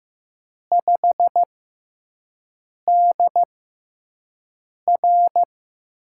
Morse code, heard as a keyed tone, about 15 wpm, spaced as for 5 wpm, 700 Hz. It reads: 5DR
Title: USA Call Signs - 4 Characters - 6x Character Spacing - 15wpm